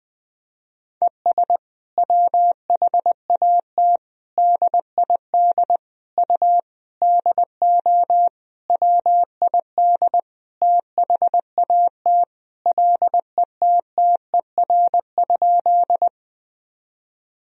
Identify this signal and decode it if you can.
Morse code, heard as a keyed tone, about 20 wpm, 700 Hz. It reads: ES WHAT DID U DO WID THAT LETTER?
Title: Sentences from Top 300 Words and Abbreviations 20wpm